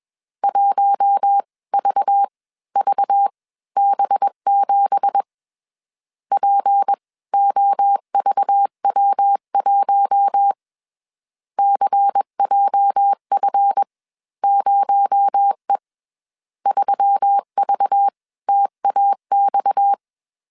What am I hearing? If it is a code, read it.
Morse code, heard as a keyed tone, about 21 wpm, spaced as for 17 wpm, 800 Hz. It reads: 1VVB7 POVW1 CJF0E 3VTAX
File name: CW / DET / 1 9 1500C S C 20wpm 2310